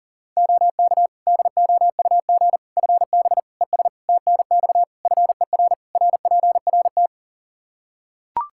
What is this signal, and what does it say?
Morse code, heard as a keyed tone, about 40 wpm, 700 Hz. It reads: OK DOUG FB ES TNX FER RPRT
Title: Practice QSO - Exchange 2 - Info - 40wpm